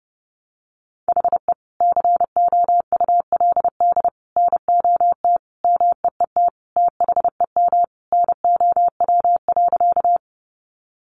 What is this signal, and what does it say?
Morse code, heard as a keyed tone, about 30 wpm, 700 Hz. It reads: HE COULD NOT MEET THEM NOW.
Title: Sentences from Top 1100 Words 30wpm